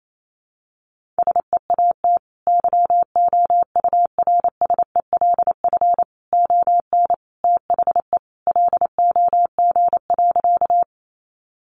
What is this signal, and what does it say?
Morse code, heard as a keyed tone, about 28 wpm, 700 Hz. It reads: SEAT YOURSELF ON THE LOG.
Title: Sentences from Top 1500 Words 28wpm